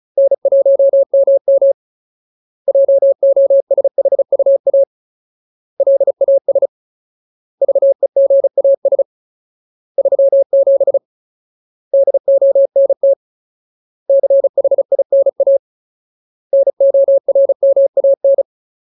Morse code, heard as a keyed tone, about 34 wpm, 550 Hz. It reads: N1MM JOSHUA LAS VEGAS 37 DONT CHINA NORMAN
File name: Ultimate Word List / 35WPM Ultimate